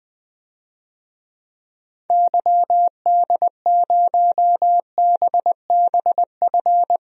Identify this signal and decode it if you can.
Morse code, heard as a keyed tone, about 20 wpm, 700 Hz. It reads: YD0BBF
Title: DXCC-100 Call Signs - 6 Characters - 20wpm